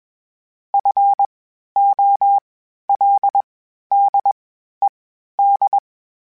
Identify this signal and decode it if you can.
Morse code, heard as a keyed tone, about 21 wpm, spaced as for 12 wpm, 800 Hz. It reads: FOLDED
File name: (spaced) / FromSMs spaced mp3s / Antennas sp@25